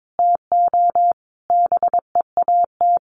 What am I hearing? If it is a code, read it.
Morse code, heard as a keyed tone, about 22 wpm, 700 Hz. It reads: TO BEAT
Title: Sentences from Top 1300 Words 22wpm